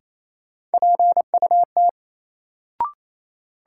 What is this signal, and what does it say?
Morse code, heard as a keyed tone, about 28 wpm, 700 Hz. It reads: PUT
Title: Sentences from Top 200 Words - New Word Drill - 28wpm